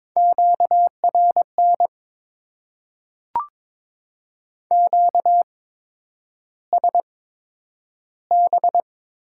Morse code, heard as a keyed tone, about 22 wpm, spaced as for 9 wpm, 700 Hz. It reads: QRN QSB
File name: Top 90 QSO Elements - 8x Character Spacing - 22wpm